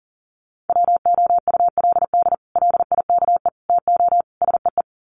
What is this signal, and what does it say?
Morse code, heard as a keyed tone, about 40 wpm, 700 Hz. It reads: WOULD LIKE TO SEE
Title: Sets of 4 Words 40wpm